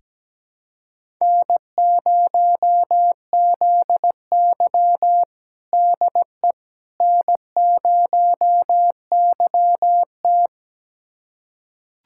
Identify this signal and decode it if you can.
Morse code, heard as a keyed tone, about 17 wpm, 700 Hz. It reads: N0ZY DE N0YT
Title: USA Callsign DE Callsign - 4 Characters - 17wpm